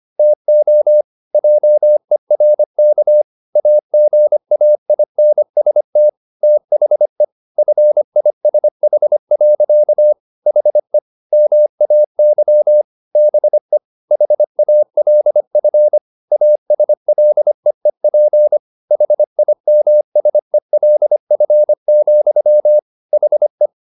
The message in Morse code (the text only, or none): TO JERK AGAINST THE FISH. HE MAY BE HALF ASLEEP HIMSELF, HE